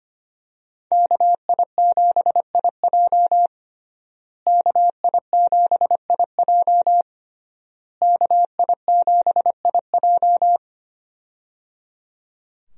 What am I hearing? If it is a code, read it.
Morse code, heard as a keyed tone, about 25 wpm, 700 Hz. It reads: KI7IJ KI7IJ KI7IJ
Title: USA Call Signs - 5 Characters - Repeated 3x - 25wpm